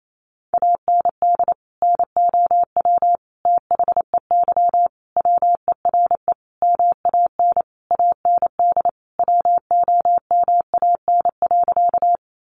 Morse code, heard as a keyed tone, about 28 wpm, 700 Hz. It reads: AND NOW THEY WERE MAN AND WOMAN.
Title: Sentences from Top 1100 Words 28wpm